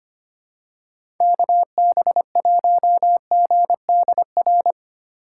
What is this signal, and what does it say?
Morse code, heard as a keyed tone, about 25 wpm, 700 Hz. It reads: KB1GDR
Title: USA Call Signs - 6 Characters - 4x Character Spacing - 25wpm